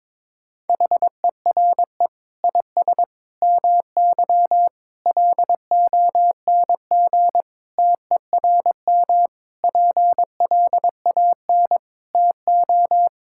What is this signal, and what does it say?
Morse code, heard as a keyed tone, about 22 wpm, 700 Hz. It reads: HERE IS MY LONG TERM PLAN TO
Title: Sentences from Top 1600 Words 22wpm